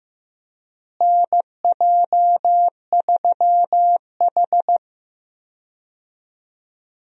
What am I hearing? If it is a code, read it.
Morse code, heard as a keyed tone, about 15 wpm, 700 Hz. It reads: NJ3H
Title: USA Call Signs - 4 Characters - Repeated 3x - 15wpm